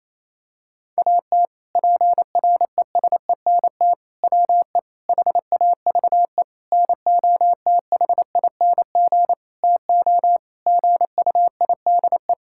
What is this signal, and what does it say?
Morse code, heard as a keyed tone, about 28 wpm, 700 Hz. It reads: AT PRESENT WE HAVE NOTHING TO GUIDE
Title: Sentences from Top 1000 Words - Review Part 5 - 28wpm